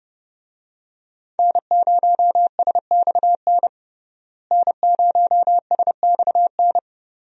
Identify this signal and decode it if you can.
Morse code, heard as a keyed tone, about 30 wpm, 700 Hz. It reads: N0SXN N0SXN
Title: USA Call Signs - 5 Characters - Repeated 2x - 30wpm